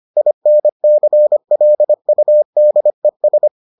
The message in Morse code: INCLUDES

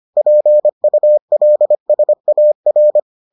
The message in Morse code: PULSAR

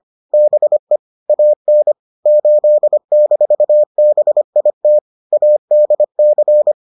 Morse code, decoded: BE AN 8-BIT ADC

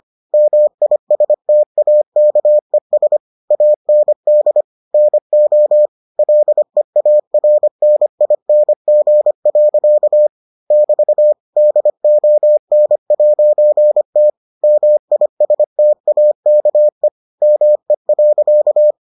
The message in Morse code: MISTAKES AND NO LEARNING. <BT> DON'T MISTAKE ME.

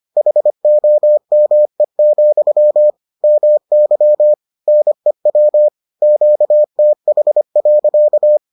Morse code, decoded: HOME, MY NEW QTH.

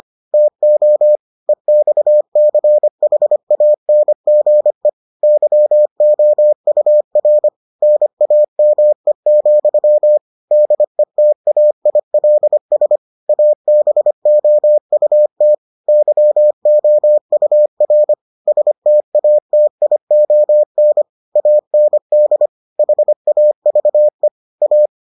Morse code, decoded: TO EXCHANGE YOUR NAME, DETAILS ABOUT YOUR STATION AND HAVE A

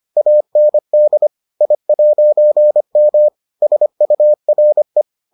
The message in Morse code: AND I'M SURE

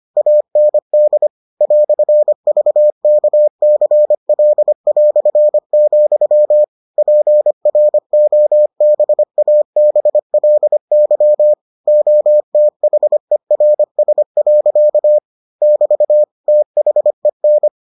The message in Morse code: AND "VKCL", PROBABLY OTHERS. <BT> THEN

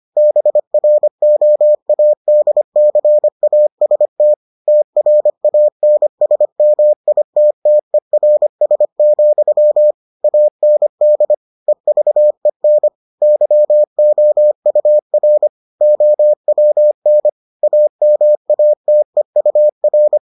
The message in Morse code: BROADCAST TRANSMITTERS, AND EVEN YOUR OWN AMATEUR

